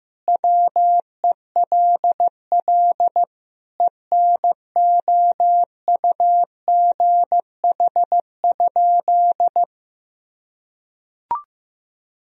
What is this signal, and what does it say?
Morse code, heard as a keyed tone, about 16 wpm, 700 Hz. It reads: WELL ENOUGH?
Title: Sentences from Top 1000 Words - Review Part 3 - 15wpm